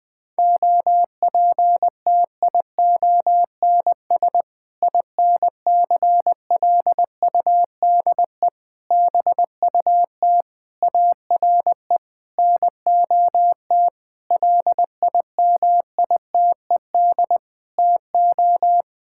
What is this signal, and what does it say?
Morse code, heard as a keyed tone, about 20 wpm, 700 Hz. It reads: OPTIONS INCLUDE BUT ARE NOT LIMITED TO
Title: Sentences with 6-Word Phrases - Spoken Beginning - 20wpm